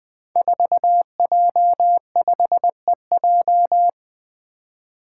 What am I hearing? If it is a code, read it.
Morse code, heard as a keyed tone, about 20 wpm, 700 Hz. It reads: 4J5EJ
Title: Five Challenging Letter-Number Combinations 20wpm